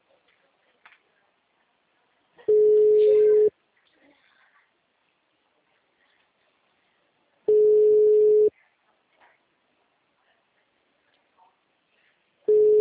Ringback tone (Europe, 425 Hz).
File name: out- -2017- - - .1583